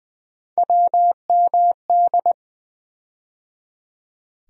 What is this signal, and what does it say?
Morse code, heard as a keyed tone, about 20 wpm, 700 Hz. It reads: WMD
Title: USA Call Sign Postfixes - 3 Letters 20wpm